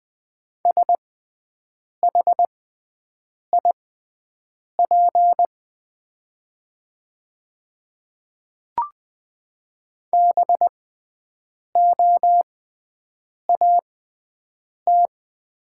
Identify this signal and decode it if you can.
Morse code, heard as a keyed tone, about 20 wpm, spaced as for 7 wpm, 700 Hz. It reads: SHIP BOAT
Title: 6x Character Spacing - Top 500 Words - 20wpm